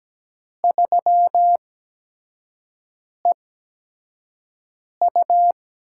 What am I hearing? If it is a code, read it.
Morse code, heard as a keyed tone, about 17 wpm, spaced as for 5 wpm, 700 Hz. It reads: 3EU
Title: USA Call Signs - 8x Character Spacing - 17wpm